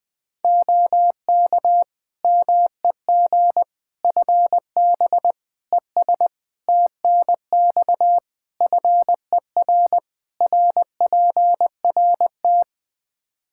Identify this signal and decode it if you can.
Morse code, heard as a keyed tone, about 20 wpm, 700 Hz. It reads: OK MEG FB ES TNX FER RPRT
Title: Practice QSO - Exchange 2 - Info - 20wpm